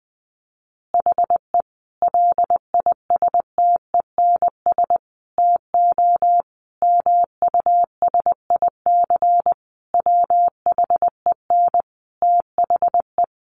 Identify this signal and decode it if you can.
Morse code, heard as a keyed tone, about 20 wpm, 700 Hz. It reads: HE LISTENS TO MUSIC WHEN THE
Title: Additional Easy Phrases 20wpm